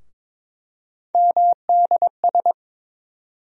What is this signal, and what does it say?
Morse code, heard as a keyed tone, about 22 wpm, 700 Hz. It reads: MDS
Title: USA Call Sign Postfixes - 3 Letters 22wpm